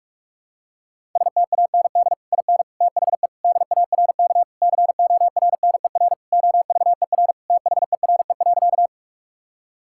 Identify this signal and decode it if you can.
Morse code, heard as a keyed tone, about 43 wpm, 700 Hz. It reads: STAND IN THE DARK CORNER OVER THERE.
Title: Sentences from Top 1600 Words 45wpm